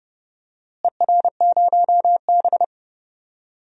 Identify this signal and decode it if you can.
Morse code, heard as a keyed tone, about 30 wpm, 700 Hz. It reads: ER0B
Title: DXCC-100 Call Signs - 4 Characters - Repeated 2x - 30wpm